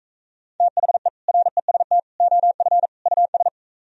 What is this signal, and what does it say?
Morse code, heard as a keyed tone, about 44 wpm, 700 Hz. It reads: THE REST OF US